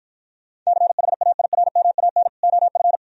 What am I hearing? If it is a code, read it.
Morse code, heard as a keyed tone, about 52 wpm, 690 Hz. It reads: CHAIRMAN OF